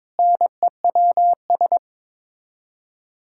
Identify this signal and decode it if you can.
Morse code, heard as a keyed tone, about 22 wpm, 700 Hz. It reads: NEWS
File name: Sets of 9 Words - 22wpm